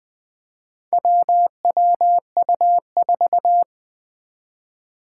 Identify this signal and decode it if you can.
Morse code, heard as a keyed tone, about 20 wpm, 700 Hz. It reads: WWU4
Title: Four Challenging Letter-Number Combinations 20wpm